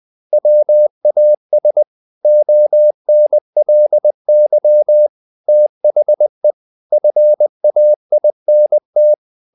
Morse code, heard as a keyed tone, about 20 wpm, 600 Hz. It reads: WAS ONLY THE FAINT
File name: the old man and the sea / the old man and the sea part 1 20wpm